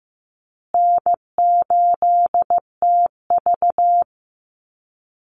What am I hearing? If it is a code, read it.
Morse code, heard as a keyed tone, about 15 wpm, 700 Hz. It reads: N8TV
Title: USA Call Signs Repeated 3x 15wpm